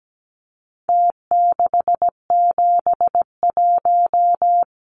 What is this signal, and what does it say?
Morse code, heard as a keyed tone, about 17 wpm, 700 Hz. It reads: T671
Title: Numbers - Contest Style - 4-Digit 17wpm